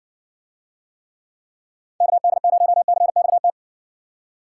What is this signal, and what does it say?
Morse code, heard as a keyed tone, about 60 wpm, 700 Hz. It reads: KD0CXT